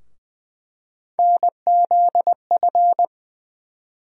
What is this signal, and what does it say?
Morse code, heard as a keyed tone, about 20 wpm, 700 Hz. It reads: NZF
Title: USA Call Sign Postfixes - 3 Letters 20wpm